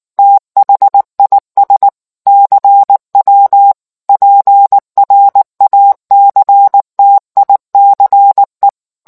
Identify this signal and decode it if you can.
Morse code, heard as a keyed tone, about 19 wpm, 800 Hz. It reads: THIS CW PRACTICE